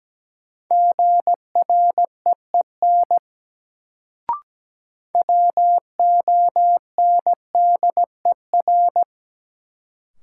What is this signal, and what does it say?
Morse code, heard as a keyed tone, about 17 wpm, 700 Hz. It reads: GREEN WONDER
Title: Sentences from Top 500 Words - New Word Drill - 17wpm